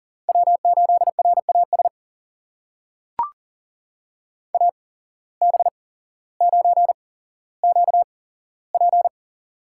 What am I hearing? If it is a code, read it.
Morse code, heard as a keyed tone, about 37 wpm, spaced as for 17 wpm, 700 Hz. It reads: W8RAS AB9QP
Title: USA Call Signs - 5 Characters - 8x Character Spacing - 40wpm